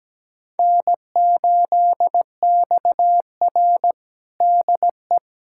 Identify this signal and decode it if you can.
Morse code, heard as a keyed tone, about 17 wpm, 700 Hz. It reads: N8XR DE
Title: USA Callsign DE Callsign - 4 Characters - 17wpm